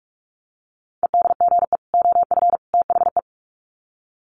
Morse code, heard as a keyed tone, about 44 wpm, 700 Hz. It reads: EDGE OF THE
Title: Sets of 3 Words 45wpm